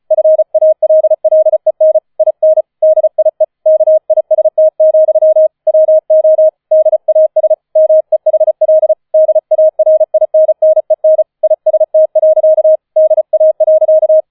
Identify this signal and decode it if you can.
Morse code, heard as a keyed tone, about 35 wpm, 630 Hz. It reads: FALLEN IN DIE KIST, WO DAS MEHL DARINNEN IST. DA.